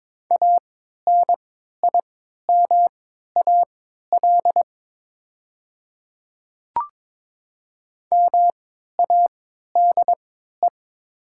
Morse code, heard as a keyed tone, about 22 wpm, spaced as for 12 wpm, 700 Hz. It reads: ANIMAL MADE
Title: 3x Character Spacing - Top 300 Words - 22wpm